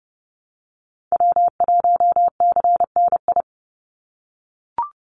Morse code, heard as a keyed tone, about 30 wpm, 700 Hz. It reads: W1CNI
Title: USA Call Signs 30wpm